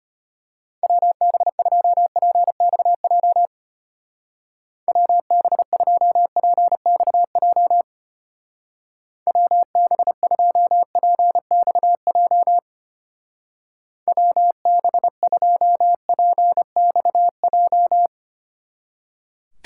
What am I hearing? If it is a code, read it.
Morse code, heard as a keyed tone, about 30 wpm, 700 Hz. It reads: WB2PXJ WB2PXJ WB2PXJ WB2PXJ
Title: Speed Racing - USA 2x3 Call Signs 25wpm